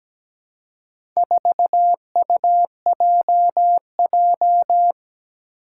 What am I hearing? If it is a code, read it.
Morse code, heard as a keyed tone, about 17 wpm, 700 Hz. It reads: 4UJJ
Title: Four Challenging Letter-Number Combinations 17wpm